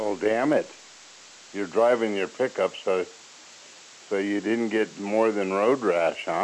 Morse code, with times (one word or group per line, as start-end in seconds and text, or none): none